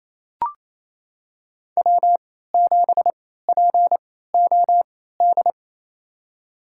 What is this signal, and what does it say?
Morse code, heard as a keyed tone, about 28 wpm, spaced as for 16 wpm, 700 Hz. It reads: W7POD
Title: USA Call Signs - 3x Character Spacing - 28wpm